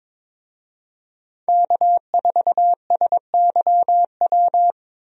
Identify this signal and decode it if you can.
Morse code, heard as a keyed tone, about 22 wpm, 700 Hz. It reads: K4SYW
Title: USA Call Signs - 5 Characters - Repeated 3x - 22wpm